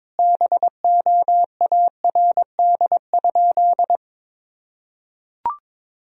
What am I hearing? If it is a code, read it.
Morse code, heard as a keyed tone, about 22 wpm, 700 Hz. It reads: BOARD?
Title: Sentences from Top 1000 Words - Review Part 4 - 22wpm